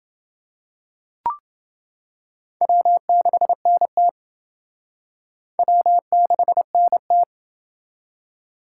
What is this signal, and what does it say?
Morse code, heard as a keyed tone, about 29 wpm, 700 Hz. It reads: W6NT W6NT